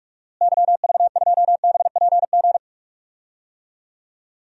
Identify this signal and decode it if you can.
Morse code, heard as a keyed tone, about 45 wpm, 700 Hz. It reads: YV2BPG
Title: DXCC-100 Call Signs - 6 Characters - Repeated 2x - 45wpm